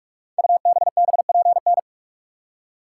Morse code, heard as a keyed tone, about 41 wpm, 700 Hz. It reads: UBBPN
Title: Code Groups - 5 Letter:Numbers - 45wpm